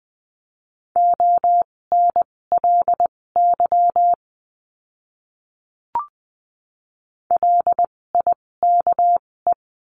Morse code, heard as a keyed tone, about 20 wpm, spaced as for 16 wpm, 700 Hz. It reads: ONLY LIKE
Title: 256 Lesson 42a - Words 12-20wpm